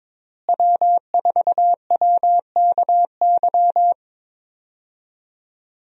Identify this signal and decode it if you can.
Morse code, heard as a keyed tone, about 22 wpm, 700 Hz. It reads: W4WKY
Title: USA Call Signs - 5 Characters - Repeated 2x - 22wpm